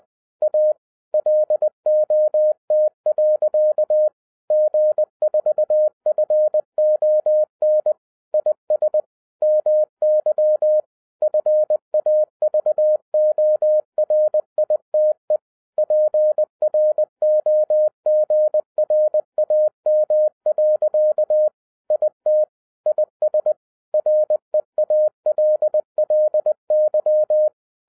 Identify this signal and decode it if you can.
Morse code, heard as a keyed tone, about 20 wpm, 610 Hz. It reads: A LOT. G4FON IS MY FAVORITE PROGRAM. IT IS REALLY
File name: 2012-01-mp3 / n05-CAME BACK TO HAM RADIO AFTER 30 YEARS OF INTERVAL-20wpm